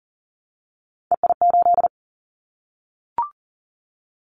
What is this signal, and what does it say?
Morse code, heard as a keyed tone, about 40 wpm, 700 Hz. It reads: EI8
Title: Callsign Prefixes for Top 50 DXCC - 40wpm